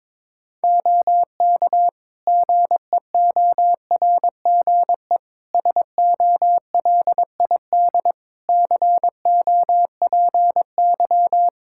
Morse code, heard as a keyed tone, about 22 wpm, 700 Hz. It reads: OK GEORGE SOLID COPY
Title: K1USN SST QSO Practice 22wpm